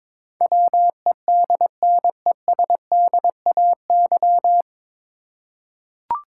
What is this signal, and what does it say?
Morse code, heard as a keyed tone, about 23 wpm, 700 Hz. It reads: WEDNESDAY
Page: Morse Code Ninja practice 22wpm